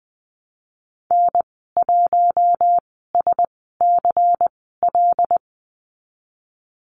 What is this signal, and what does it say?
Morse code, heard as a keyed tone, about 20 wpm, spaced as for 15 wpm, 700 Hz. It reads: N1SCL
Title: 128 Lesson 21g - Number 6 - Call Signs 20wpm